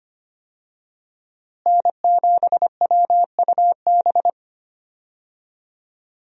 Morse code, heard as a keyed tone, about 25 wpm, 700 Hz. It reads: N7WUB
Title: USA Call Signs - 5 Characters - Repeated 2x - 25wpm